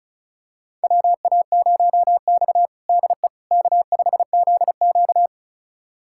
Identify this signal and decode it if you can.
Morse code, heard as a keyed tone, about 36 wpm, 700 Hz. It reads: WA0X DE K5ZQ
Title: USA Callsign DE Callsign - 4 Characters - 35wpm